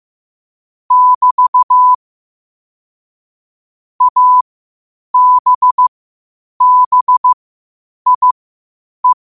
Morse code, heard as a keyed tone, about 15 wpm, spaced as for 8 wpm, 1000 Hz. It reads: <BT> ABBIE